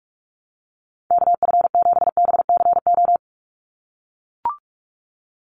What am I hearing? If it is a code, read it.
Morse code, heard as a keyed tone, about 44 wpm, 700 Hz. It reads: KF7BCO
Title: USA Call Signs 45wpm